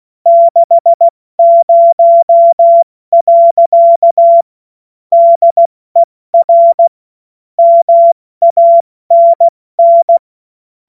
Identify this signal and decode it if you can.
Morse code, heard as a keyed tone, about 16 wpm, spaced as for 14 wpm, 680 Hz. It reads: 60. DER MANN
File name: Morsekurs / Witze / Witz 11. Mai 2021, 70 F80